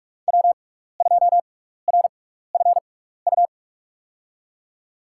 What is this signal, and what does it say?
Morse code, heard as a keyed tone, about 46 wpm, spaced as for 15 wpm, 700 Hz. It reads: W2RFU
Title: USA Call Signs - 5 Characters - 6x Character Spacing - 45wpm